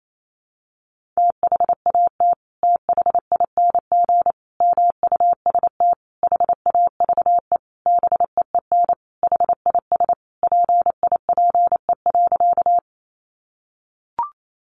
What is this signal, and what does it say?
Morse code, heard as a keyed tone, about 28 wpm, 700 Hz. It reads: THAT THING MUST HAVE BEEN HIS PIPE.
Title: Sentences from Top 1500 Words 28wpm